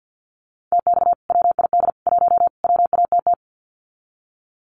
Morse code, heard as a keyed tone, about 50 wpm, 700 Hz. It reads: TX WID 1 WATT
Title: Common QSO Phrases 50wpm